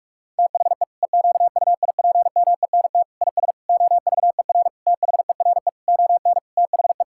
Morse code, heard as a keyed tone, about 41 wpm, 700 Hz. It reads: THE EQUIPMENT IS OVER THERE ON THE
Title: Sentences from Top 1000 Words - Review Part 5 - 45wpm